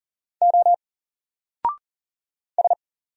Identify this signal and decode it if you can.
Morse code, heard as a keyed tone, about 40 wpm, 700 Hz. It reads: O S
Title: Single Letter-Number 40wpm